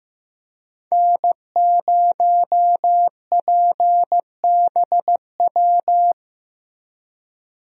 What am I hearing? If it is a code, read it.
Morse code, heard as a keyed tone, about 15 wpm, 700 Hz. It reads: N0PBW
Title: USA Call Signs - 5 Characters - Repeated 2x - 15wpm